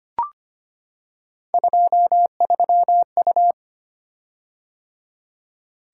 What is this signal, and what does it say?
Morse code, heard as a keyed tone, about 25 wpm, 700 Hz. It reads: 23U
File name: Three Letter-Number Combinations - 25wpm